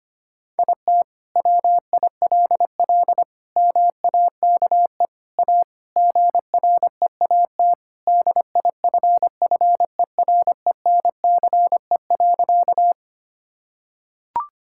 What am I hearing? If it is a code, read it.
Morse code, heard as a keyed tone, about 25 wpm, 700 Hz. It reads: IT WILL MAKE A GREAT DIFFERENCE.
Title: Sentences from Top 1000 Words - Review Part 5 - 25wpm